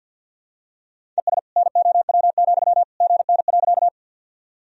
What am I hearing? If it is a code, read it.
Morse code, heard as a keyed tone, about 53 wpm, 700 Hz. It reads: ES NOW, GN.
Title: Sentences from Top 300 Words and Abbreviations 50wpm